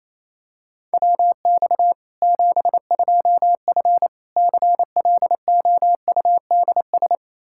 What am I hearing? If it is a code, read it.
Morse code, heard as a keyed tone, about 27 wpm, 700 Hz. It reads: WX 72F CLOUDS
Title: K1USN SST QSO Practice 28wpm